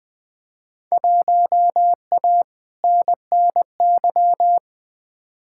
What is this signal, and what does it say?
Morse code, heard as a keyed tone, about 20 wpm, 700 Hz. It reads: 1A NNY
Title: Field Day Exchanges Encore - 20wpm